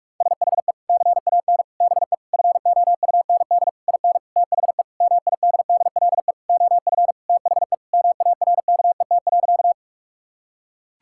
Morse code, heard as a keyed tone, about 46 wpm, 700 Hz. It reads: SHE CAN BE FOUND IN THE MIDDLE OF THE MARKET.
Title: Sentences from Top 1000 Words - Review Part 6 - 45wpm